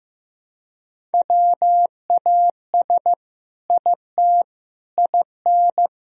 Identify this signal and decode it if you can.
Morse code, heard as a keyed tone, about 15 wpm, 700 Hz. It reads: WAS IT IN
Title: Sentences from Top 1000 Words - Review Part 2 - 15wpm